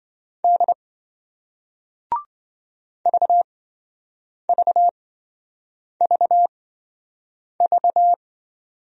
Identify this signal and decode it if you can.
Morse code, heard as a keyed tone, about 26 wpm, spaced as for 7 wpm, 700 Hz. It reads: D VVVV